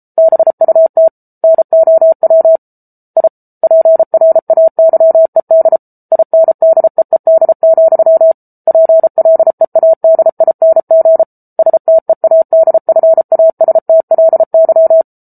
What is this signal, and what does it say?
Morse code, heard as a keyed tone, about 34 wpm, 650 Hz. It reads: BUT NOW I PRAYED INDEED, PLEADING STEADFASTLY